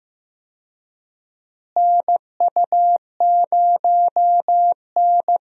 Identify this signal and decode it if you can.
Morse code, heard as a keyed tone, about 15 wpm, 700 Hz. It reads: NU0N
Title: USA Call Signs - 4 Characters - Repeated 2x - 15wpm